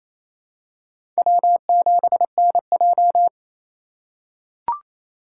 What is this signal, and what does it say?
Morse code, heard as a keyed tone, about 28 wpm, 700 Hz. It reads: W7NJ
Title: USA Call Signs - 4 Characters - 2x Character Spacing - 28wpm